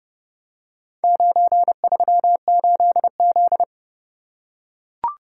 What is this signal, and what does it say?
Morse code, heard as a keyed tone, about 28 wpm, 700 Hz. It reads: 938Z